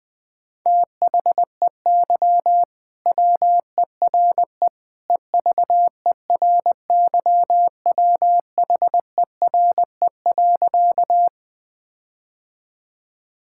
Morse code, heard as a keyed tone, about 20 wpm, 700 Hz. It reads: THEY WERE EVERYWHERE.
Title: Sentences from Top 1600 Words 20wpm